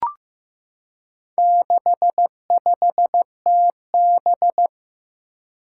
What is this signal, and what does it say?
Morse code, heard as a keyed tone, about 15 wpm, 700 Hz. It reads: 65TB
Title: Four Challenging Letter-Number Combinations 15wpm